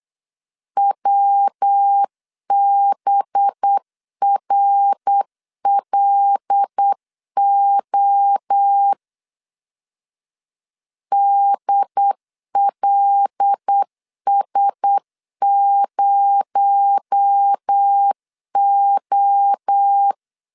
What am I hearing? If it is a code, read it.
Morse code, heard as a keyed tone, about 8 wpm, 800 Hz. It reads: WBRLO DLS0O
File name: LEZIONI / 6 PKX0 / 6 PKX0 8wpm 3341